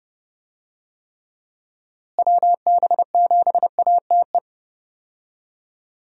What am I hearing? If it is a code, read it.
Morse code, heard as a keyed tone, about 30 wpm, 700 Hz. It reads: WB7ATE
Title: USA Call Signs - 6 Characters - Repeated 2x - 30wpm